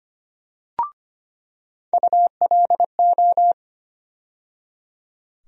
Morse code, heard as a keyed tone, about 25 wpm, 700 Hz. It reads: ULO
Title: USA Call Sign Postfixes - 3 Letters 25wpm